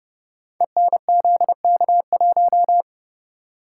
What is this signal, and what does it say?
Morse code, heard as a keyed tone, about 30 wpm, 700 Hz. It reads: ENZK1
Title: Code Groups - 5 Letter:Numbers - 30wpm